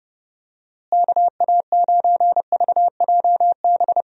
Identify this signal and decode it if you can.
Morse code, heard as a keyed tone, about 32 wpm, 700 Hz. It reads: KA9VJB
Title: USA Call Signs - 6 Characters - Repeated 2x - 30wpm